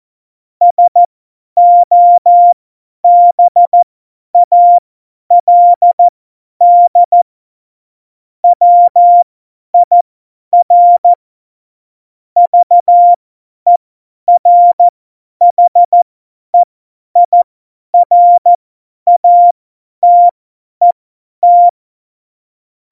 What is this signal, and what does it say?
Morse code, heard as a keyed tone, about 14 wpm, spaced as for 10 wpm, 700 Hz. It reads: SOBALD WIR VERHEIRATET
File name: Morsekurs / Witze / Witz 27. August 2019, 50 F70